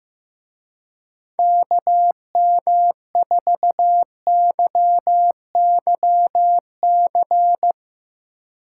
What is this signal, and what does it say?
Morse code, heard as a keyed tone, about 15 wpm, 700 Hz. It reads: KM4YYC
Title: Speed Racing - USA Call Signs 15wpm